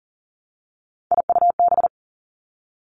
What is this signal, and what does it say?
Morse code, heard as a keyed tone, about 40 wpm, 700 Hz. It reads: IUB
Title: Three Challenging Letter-Number Combinations 40wpm